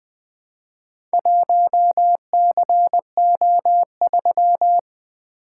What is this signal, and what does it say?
Morse code, heard as a keyed tone, about 20 wpm, 700 Hz. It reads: 1CO3
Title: Four Letter-Number Combinations - 20wpm